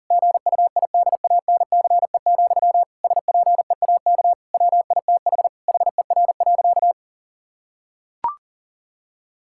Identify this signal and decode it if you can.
Morse code, heard as a keyed tone, about 39 wpm, 700 Hz. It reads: GUIDANCE, SPEAK WITH HER.
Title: Sentences from Top 1600 Words 40wpm